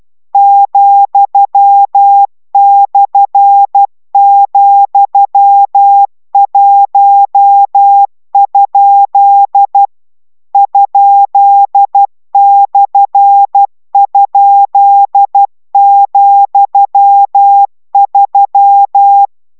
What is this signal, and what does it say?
Morse code, heard as a keyed tone, about 12 wpm, 800 Hz. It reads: ,/,1? ?/?,3